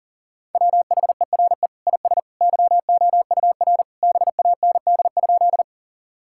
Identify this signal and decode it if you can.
Morse code, heard as a keyed tone, about 40 wpm, 700 Hz. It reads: WHERE IS YOUR BAND?
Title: Sentences from Top 1600 Words 40wpm